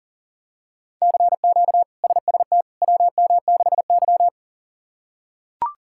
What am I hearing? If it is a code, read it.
Morse code, heard as a keyed tone, about 38 wpm, 700 Hz. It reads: CQ SST WM6Y
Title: K1USN SST QSO Practice - 40wpm